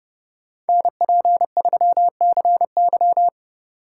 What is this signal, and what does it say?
Morse code, heard as a keyed tone, about 33 wpm, 700 Hz. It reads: NP3CY